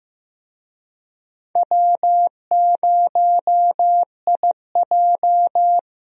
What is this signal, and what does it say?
Morse code, heard as a keyed tone, about 15 wpm, 700 Hz. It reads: W0IJ
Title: USA Call Signs - 4 Characters - Repeated 2x - 15wpm